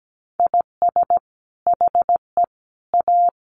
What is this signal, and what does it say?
Morse code, heard as a keyed tone, about 17 wpm, 700 Hz. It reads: IS HE A
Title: Sentences from Top 200 Words 17wpm